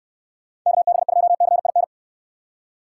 Morse code, heard as a keyed tone, about 69 wpm, 700 Hz. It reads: KB2CIN